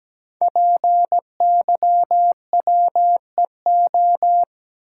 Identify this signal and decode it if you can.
Morse code, heard as a keyed tone, about 17 wpm, 700 Hz. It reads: PYWEO
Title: Code Groups - 5 Letters - 17wpm